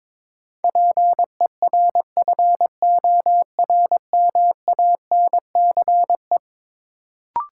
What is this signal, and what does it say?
Morse code, heard as a keyed tone, about 21 wpm, 700 Hz. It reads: PERFORMANCE
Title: Sentences from Top 1000 Words - New Word Drill - 22wpm